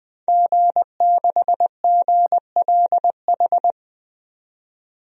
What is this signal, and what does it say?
Morse code, heard as a keyed tone, about 21 wpm, 700 Hz. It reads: G6GLH
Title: DXCC-100 Call Signs - 5 Characters - 20wpm